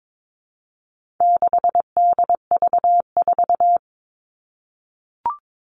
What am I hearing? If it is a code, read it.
Morse code, heard as a keyed tone, about 22 wpm, 700 Hz. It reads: 6DV4
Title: All Cut Numbers - 4-Digit 22wpm